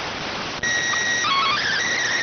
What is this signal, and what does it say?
SSTV (slow-scan television), PD 290.